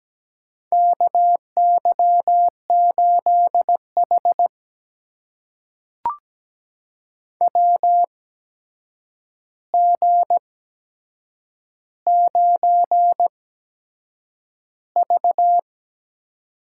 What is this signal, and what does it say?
Morse code, heard as a keyed tone, about 17 wpm, spaced as for 7 wpm, 700 Hz. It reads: KY8H WG9V
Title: USA Call Signs - 4 Characters - 8x Character Spacing - 17wpm